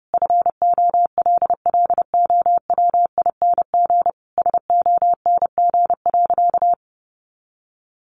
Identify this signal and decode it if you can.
Morse code, heard as a keyed tone, about 30 wpm, 700 Hz. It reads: FOLLOWING SONG.
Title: Sentences from Top 900 Words 30wpm